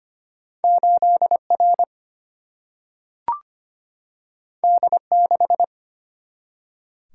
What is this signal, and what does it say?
Morse code, heard as a keyed tone, about 25 wpm, 700 Hz. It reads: 8R D6
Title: Single-Letter Single-Number 25wpm